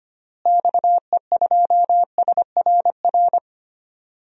Morse code, heard as a keyed tone, about 25 wpm, 700 Hz. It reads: XE2SRR
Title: DXCC-100 Call Signs - 6 Characters - Repeated 3x - 25wpm